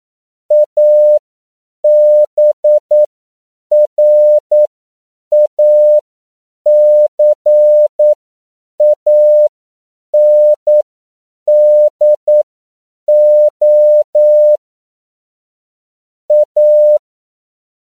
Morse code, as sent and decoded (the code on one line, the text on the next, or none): .- -... .-. .- -.-. .- -. -.. --- / .-
ABRACANDO A